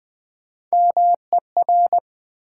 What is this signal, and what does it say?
Morse code, heard as a keyed tone, about 20 wpm, 700 Hz. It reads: MER
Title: Build A Word - 1 Letter At A Time - 5-Letter Words - 20wpm